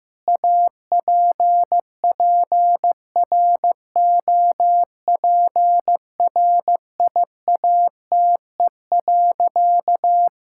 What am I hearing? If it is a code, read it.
Morse code, heard as a keyed tone, about 15 wpm, 700 Hz. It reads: APPROPRIATE.
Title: Sentences from Top 1000 Words - Review Part 5 - 15wpm